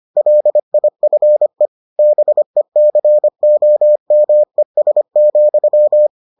Morse code, heard as a keyed tone, about 25 wpm, 600 Hz. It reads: LIFE BECOMES,